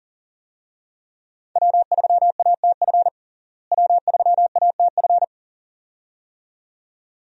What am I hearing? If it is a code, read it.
Morse code, heard as a keyed tone, about 40 wpm, 700 Hz. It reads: W3ATF W3ATF
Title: USA Call Signs - 5 Characters - Repeated 2x - 40wpm